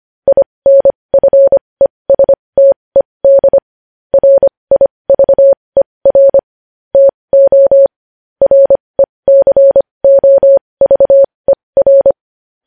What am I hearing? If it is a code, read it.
Morse code, heard as a keyed tone, about 25 wpm, spaced as for 20 wpm, 550 Hz. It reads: INFESTED RIVER TO RECOVER
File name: podcast / News headlines-1x-25x20